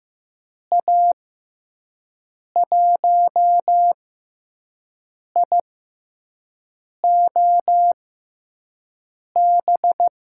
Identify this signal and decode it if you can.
Morse code, heard as a keyed tone, about 15 wpm, spaced as for 5 wpm, 700 Hz. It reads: A1IOB
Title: USA Call Signs - 6 Characters - 6x Character Spacing - 15wpm